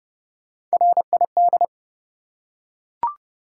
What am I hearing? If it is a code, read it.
Morse code, heard as a keyed tone, about 30 wpm, 700 Hz. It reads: RID
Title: Three Letter Words - 30wpm